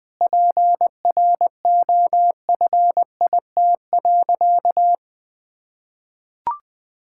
Morse code, heard as a keyed tone, about 20 wpm, 700 Hz. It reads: PROFIT.
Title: Sentences from Top 1000 Words - Review Part 5 - 20wpm